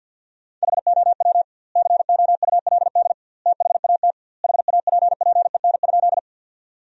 Morse code, heard as a keyed tone, about 50 wpm, 700 Hz. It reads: HOW COULD THAT HAPPEN?
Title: 2x Word-Spacing - Sentences from Top 400 Words - 50wpm